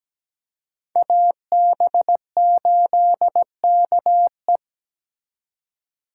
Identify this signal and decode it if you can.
Morse code, heard as a keyed tone, about 20 wpm, 700 Hz. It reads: AB8KE
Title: Speed Racing - USA 2x2 Call Signs 17wpm